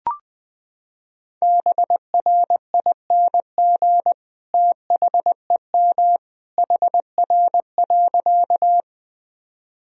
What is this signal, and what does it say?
Morse code, heard as a keyed tone, about 20 wpm, 700 Hz. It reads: BRING THEM HR.
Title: Sentences from Top 500 Words and Abbreviations 20wpm